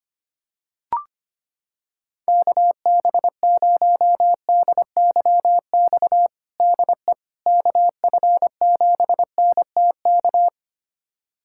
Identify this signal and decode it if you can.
Morse code, heard as a keyed tone, about 25 wpm, 700 Hz. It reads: KB0DYX DE KF7NTK
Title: USA Callsign DE Callsign - 6 Characters - 25wpm